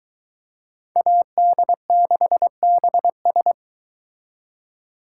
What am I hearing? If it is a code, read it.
Morse code, heard as a keyed tone, about 23 wpm, 700 Hz. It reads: AD6BS